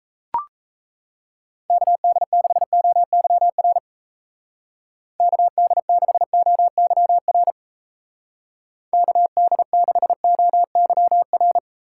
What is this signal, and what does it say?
Morse code, heard as a keyed tone, about 35 wpm, 700 Hz. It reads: KD6OYR KD6OYR KD6OYR
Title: Speed Racing - USA 2x3 Call Signs 28wpm